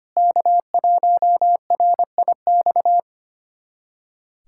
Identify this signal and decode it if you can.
Morse code, heard as a keyed tone, about 25 wpm, 700 Hz. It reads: K1RIX
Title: USA Callsign DE Callsign 25wpm